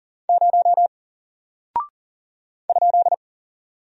Morse code, heard as a keyed tone, about 39 wpm, 700 Hz. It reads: 0 ?
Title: Single Character (Letter:Number:Punctuation:Prosign) 40wpm